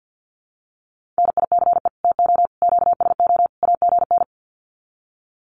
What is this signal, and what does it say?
Morse code, heard as a keyed tone, about 50 wpm, 700 Hz. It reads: NICE TO QSO AGN
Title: Common QSO Phrases 50wpm